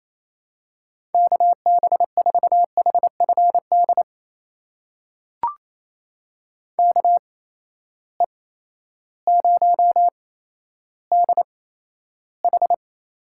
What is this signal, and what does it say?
Morse code, heard as a keyed tone, about 27 wpm, spaced as for 13 wpm, 700 Hz. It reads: KB4HFD KE0DH
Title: USA Call Signs - 6 Characters - 8x Character Spacing - 28wpm